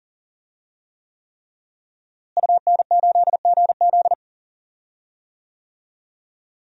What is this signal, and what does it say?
Morse code, heard as a keyed tone, about 40 wpm, 700 Hz. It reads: UN8GZ
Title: DXCC-100 Call Signs - 5 Characters - Repeated 3x - 40wpm